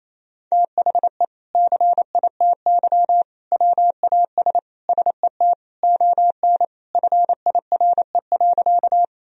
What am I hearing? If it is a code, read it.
Morse code, heard as a keyed tone, about 27 wpm, 700 Hz. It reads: THE CITY WAS SET ON FIRE.
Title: Sentences from Top 1000 Words - Review Part 1 - 28wpm